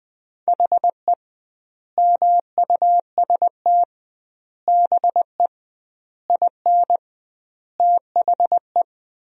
Morse code, heard as a keyed tone, about 20 wpm, 700 Hz. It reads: HE MUST BE IN THE
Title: 2x Word-Spacing - Sentences from Top 500 Words - 20wpm